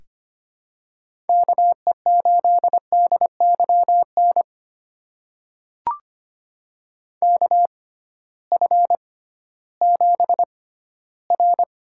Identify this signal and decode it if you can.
Morse code, heard as a keyed tone, about 25 wpm, spaced as for 15 wpm, 700 Hz. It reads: KE8DYN KF7R
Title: USA Call Signs - 6 Characters - 6x Character Spacing - 25wpm